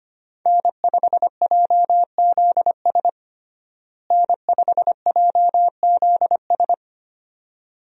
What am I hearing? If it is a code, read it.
Morse code, heard as a keyed tone, about 25 wpm, 700 Hz. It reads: N5JZS N5JZS
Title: USA Call Signs - 5 Characters - Repeated 2x - 25wpm